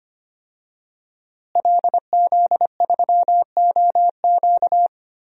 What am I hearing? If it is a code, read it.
Morse code, heard as a keyed tone, about 25 wpm, 700 Hz. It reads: LZ3OQ
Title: DXCC-100 Call Signs - 5 Characters - Repeated 3x - 25wpm